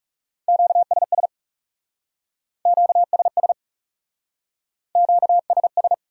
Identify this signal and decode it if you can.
Morse code, heard as a keyed tone, about 39 wpm, 700 Hz. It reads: QSS QSS QSS